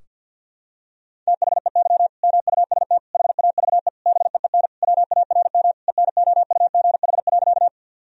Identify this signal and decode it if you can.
Morse code, heard as a keyed tone, about 50 wpm, 700 Hz. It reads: THEY MUST HAVE BEEN WARM ENOUGH.
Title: Sentences from Top 1000 Words - Review Part 2 - 50wpm